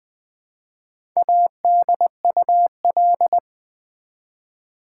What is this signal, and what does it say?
Morse code, heard as a keyed tone, about 20 wpm, 700 Hz. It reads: ADUL
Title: Build A Word - 1 Letter At A Time - 5-Letter Words - 20wpm